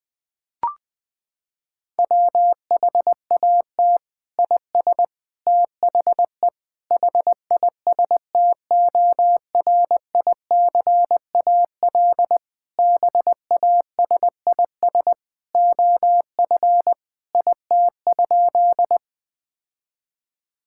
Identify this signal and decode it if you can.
Morse code, heard as a keyed tone, about 21 wpm, 700 Hz. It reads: WHAT IS THE HISTORICAL BASIS OF IT?
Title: Sentences from Top 1000 Words - Review Part 5 - 20wpm